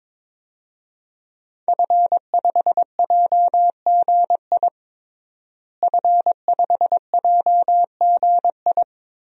Morse code, heard as a keyed tone, about 22 wpm, 700 Hz. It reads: F5JGI F5JGI
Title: DXCC-100 Call Signs - 5 Characters - Repeated 2x - 22wpm